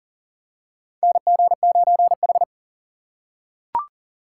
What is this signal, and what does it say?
Morse code, heard as a keyed tone, about 40 wpm, 700 Hz. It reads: NG9H
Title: USA Call Signs - 4 Characters - 4x Character Spacing - 40wpm